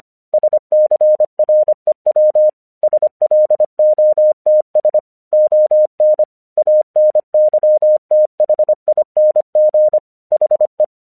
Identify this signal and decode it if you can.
Morse code, heard as a keyed tone, about 25 wpm, 610 Hz. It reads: SCREW SLOTS ON ANYTHING HE
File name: 1 English 25wpm / AQC11